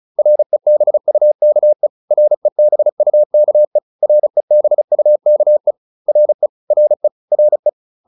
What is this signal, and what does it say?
Morse code, heard as a keyed tone, about 34 wpm, 600 Hz. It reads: REBUKE REBUKE REBUKE RE RE RE
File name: RE6 35